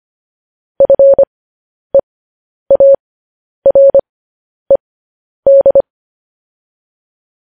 Morse code, heard as a keyed tone, about 25 wpm, spaced as for 10 wpm, 550 Hz. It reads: FEARED